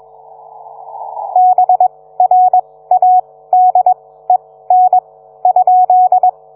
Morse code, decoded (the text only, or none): BRADEN ?